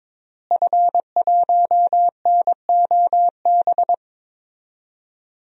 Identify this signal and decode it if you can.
Morse code, heard as a keyed tone, about 22 wpm, 700 Hz. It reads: F1NOB
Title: DXCC-100 Call Signs - 5 Characters - Repeated 3x - 22wpm